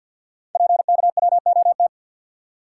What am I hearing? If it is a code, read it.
Morse code, heard as a keyed tone, about 51 wpm, 700 Hz. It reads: PKWOT
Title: Code Groups - 5 Letters - 50wpm